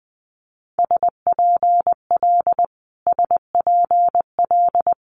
Morse code, heard as a keyed tone, about 20 wpm, 700 Hz. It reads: SPL SPL
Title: Multi-letter Phonemes Repeated 3x 20wpm